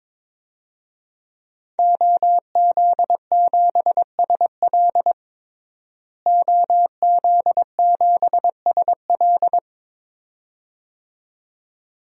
Morse code, heard as a keyed tone, about 22 wpm, 700 Hz. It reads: OZ7SL OZ7SL
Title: DXCC-100 Call Signs - 5 Characters - Repeated 2x - 22wpm